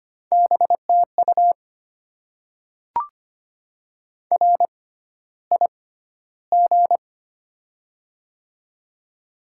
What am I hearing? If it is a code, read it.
Morse code, heard as a keyed tone, about 25 wpm, spaced as for 13 wpm, 700 Hz. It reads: BTU RIG
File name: Top 20 QSO Elements - 6x Character Spacing - 25wpm